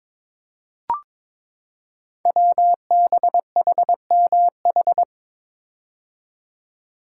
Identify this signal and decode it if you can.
Morse code, heard as a keyed tone, about 22 wpm, 700 Hz. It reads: WBHMH
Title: Code Groups - 5 Letter:Numbers - 22wpm